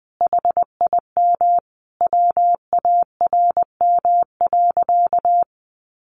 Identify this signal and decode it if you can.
Morse code, heard as a keyed tone, about 20 wpm, 700 Hz. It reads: HIM WARM.
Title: Sentences from Top 1500 Words 20wpm